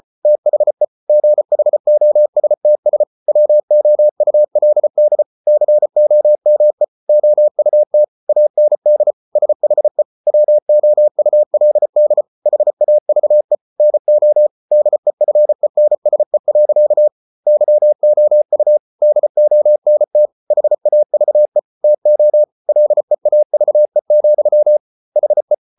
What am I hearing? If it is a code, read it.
Morse code, heard as a keyed tone, about 34 wpm, 600 Hz. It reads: THE GHOSTS WOULD COME OUT AND SHE WOULD HAVE NO DEFENSE. YOU DONT HAVE TO LEAVE, HE